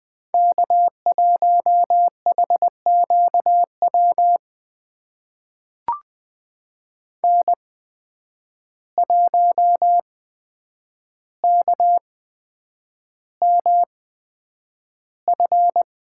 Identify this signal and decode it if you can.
Morse code, heard as a keyed tone, about 20 wpm, spaced as for 9 wpm, 700 Hz. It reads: K1HQW N1KMF
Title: USA Call Signs - 5 Characters - 8x Character Spacing - 20wpm